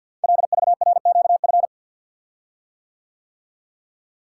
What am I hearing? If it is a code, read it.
Morse code, heard as a keyed tone, about 48 wpm, 690 Hz. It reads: LVRQF
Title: Code Groups - 5 Letters - 50wpm